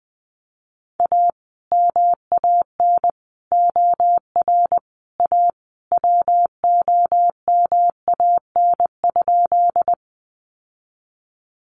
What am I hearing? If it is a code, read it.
Morse code, heard as a keyed tone, about 20 wpm, 700 Hz. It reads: A MAN OR A WOMAN?
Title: Sentences from Top 1100 Words 20wpm